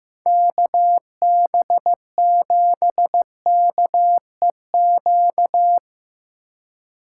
Morse code, heard as a keyed tone, about 17 wpm, 700 Hz. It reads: KB7KEQ
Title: Speed Racing - USA 2x3 Call Signs 15wpm